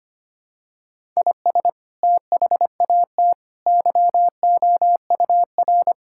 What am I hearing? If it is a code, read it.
Morse code, heard as a keyed tone, about 25 wpm, 700 Hz. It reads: IS THAT YOUR
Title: Sentences from Top 1000 Words - Review Part 6 - 25wpm